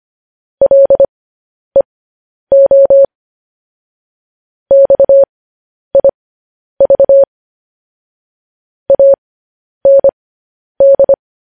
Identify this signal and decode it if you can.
Morse code, heard as a keyed tone, about 25 wpm, spaced as for 10 wpm, 550 Hz. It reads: LEO XIV AND
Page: news-headlines-in-morse-code-at-10-wpm